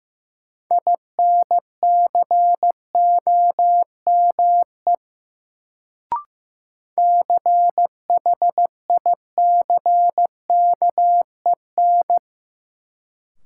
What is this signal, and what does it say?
Morse code, heard as a keyed tone, about 16 wpm, 700 Hz. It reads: INCOME CHICKEN
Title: Sentences from Top 1000 Words - Review Part 5 - New Word Drill - 15wpm